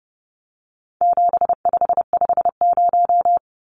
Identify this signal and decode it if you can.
Morse code, heard as a keyed tone, about 30 wpm, 700 Hz. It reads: 7550
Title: Numbers - 4-Digit 30wpm